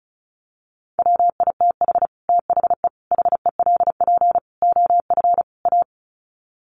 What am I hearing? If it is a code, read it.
Morse code, heard as a keyed tone, about 34 wpm, 700 Hz. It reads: WITH THE HELP OF A
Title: Sets of 5 Words 35wpm